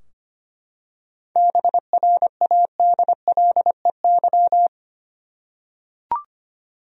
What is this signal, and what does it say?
Morse code, heard as a keyed tone, about 25 wpm, 700 Hz. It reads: BRADLEY
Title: USA Common Names - 25wpm